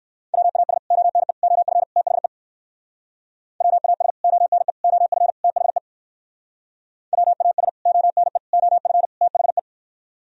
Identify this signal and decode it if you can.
Morse code, heard as a keyed tone, about 58 wpm, 700 Hz. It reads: WAS ONE OF THE WAS ONE OF THE WAS ONE OF THE